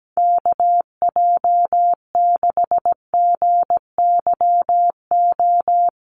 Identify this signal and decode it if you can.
Morse code, heard as a keyed tone, about 17 wpm, 700 Hz. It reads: KJ6GYO
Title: USA Call Signs Repeated 2x 17wpm